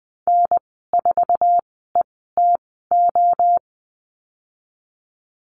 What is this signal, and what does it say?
Morse code, heard as a keyed tone, about 20 wpm, spaced as for 15 wpm, 700 Hz. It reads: N4ETO
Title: 055 Lesson 10g - Letter r - Call Signs 20wpm